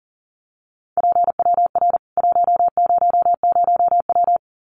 Morse code, heard as a keyed tone, about 40 wpm, 700 Hz. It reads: PWR 100W
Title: Common QSO Phrases 40wpm